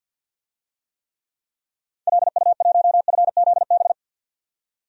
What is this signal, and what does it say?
Morse code, heard as a keyed tone, about 46 wpm, 700 Hz. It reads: LU1FZB
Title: DXCC-100 Call Signs - 6 Characters - 50wpm